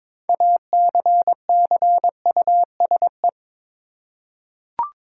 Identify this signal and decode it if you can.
Morse code, heard as a keyed tone, about 22 wpm, 700 Hz. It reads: ACCUSE
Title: Six Letter Words - 22wpm